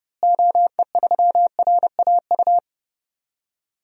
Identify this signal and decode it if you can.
Morse code, heard as a keyed tone, about 30 wpm, 700 Hz. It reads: OE3RAU
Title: DXCC-100 Call Signs - 6 Characters - Repeated 3x - 30wpm